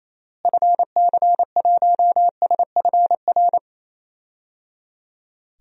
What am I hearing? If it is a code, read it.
Morse code, heard as a keyed tone, about 28 wpm, 700 Hz. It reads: FC1SFR
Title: DXCC-100 Call Signs - 6 Characters - Repeated 3x - 28wpm